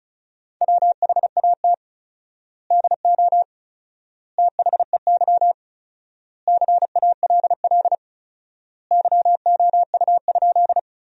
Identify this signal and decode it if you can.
Morse code, heard as a keyed tone, about 36 wpm, 700 Hz. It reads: WHAT DO THEY CALL YOU?
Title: 4x Word-Spacing - Sentences from Top 100 Words - 35wpm